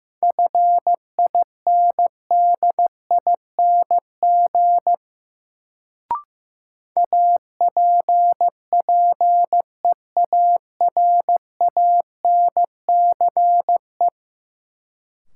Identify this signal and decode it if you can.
Morse code, heard as a keyed tone, about 15 wpm, 700 Hz. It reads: FINDING APPEARANCE
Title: Sentences from Top 1100 Words - New Word Drill - 15wpm